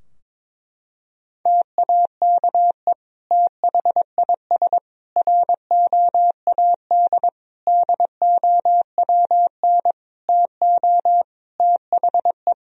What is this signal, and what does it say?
Morse code, heard as a keyed tone, about 22 wpm, 700 Hz. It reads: TAKE THIS ROAD DOWN TO THE
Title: Sentences from Top 1000 Words - Review Part 4 - 22wpm